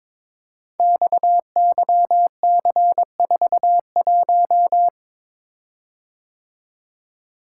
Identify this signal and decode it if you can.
Morse code, heard as a keyed tone, about 22 wpm, 700 Hz. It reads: XYC41
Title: Code Groups - 5 Letter:Numbers - 22wpm